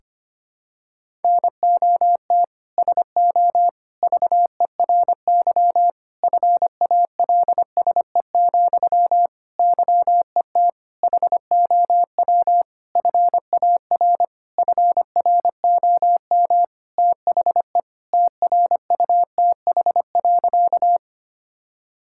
Morse code, heard as a keyed tone, about 25 wpm, 700 Hz. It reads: NOT SO VERY FALSE, YET HOW FAR FROM THE TRUTH.
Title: Sentences from Top 1600 Words 25wpm